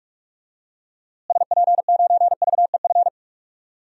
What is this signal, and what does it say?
Morse code, heard as a keyed tone, about 42 wpm, 700 Hz. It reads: SP9VEF
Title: DXCC-100 Call Signs - 6 Characters - Repeated 2x - 45wpm